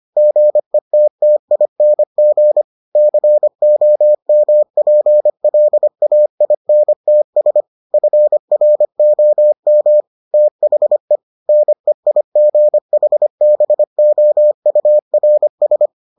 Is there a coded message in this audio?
Yes: Morse code from 0.2 to 15.9 s, heard as a keyed tone, about 25 wpm, 600 Hz. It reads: GETTING COMPLAINTS FROM THE NEIGHBOURS